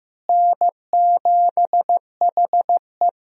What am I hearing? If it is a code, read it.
Morse code, heard as a keyed tone, about 15 wpm, 700 Hz. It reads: N7HE
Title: USA Call Signs - 4 Characters - Repeated 3x - 15wpm